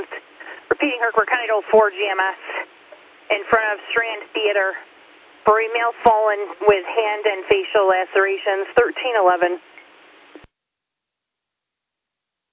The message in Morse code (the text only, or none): none